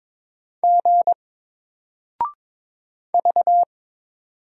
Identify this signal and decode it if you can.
Morse code, heard as a keyed tone, about 22 wpm, 700 Hz. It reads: G V
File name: Single-Letter-Number-22wpm